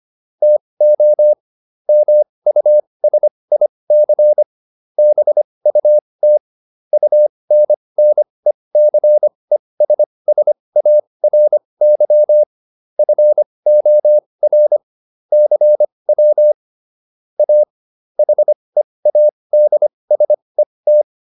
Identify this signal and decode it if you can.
Morse code, heard as a keyed tone, about 25 wpm, spaced as for 20 wpm, 600 Hz. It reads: TO MUSIC BUT UNNECESSARY FOR CW A HEADSET